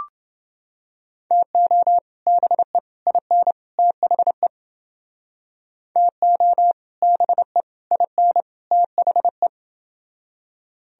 Morse code, heard as a keyed tone, about 30 wpm, 700 Hz. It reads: TO BE IN THE TO BE IN THE